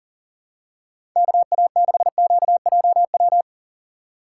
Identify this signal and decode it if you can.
Morse code, heard as a keyed tone, about 37 wpm, 700 Hz. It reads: KA6QJW
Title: USA Call Signs - 6 Characters - 4x Character Spacing - 40wpm